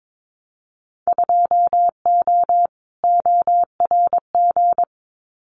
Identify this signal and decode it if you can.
Morse code, heard as a keyed tone, about 22 wpm, 700 Hz. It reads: 2O ORG
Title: Winter Field Day Exchanges 22wpm